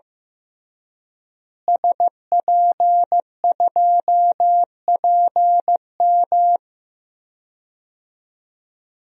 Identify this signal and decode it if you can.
Morse code, heard as a keyed tone, about 16 wpm, 700 Hz. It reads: SP2PM
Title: DXCC-100 Call Signs - 5 Characters - Repeated 3x - 15wpm